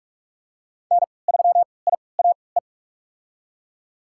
Morse code, heard as a keyed tone, about 43 wpm, spaced as for 25 wpm, 700 Hz. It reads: N3IAE
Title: USA Call Signs - 5 Characters - 3x Character Spacing - 45wpm